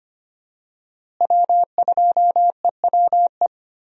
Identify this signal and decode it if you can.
Morse code, heard as a keyed tone, about 25 wpm, 700 Hz. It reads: W2EWE
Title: USA Call Signs - 5 Characters - Repeated 3x - 25wpm